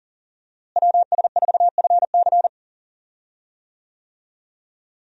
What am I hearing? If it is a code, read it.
Morse code, heard as a keyed tone, about 39 wpm, 700 Hz. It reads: WS4FC
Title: USA Call Signs - 5 Characters - Repeated 3x - 40wpm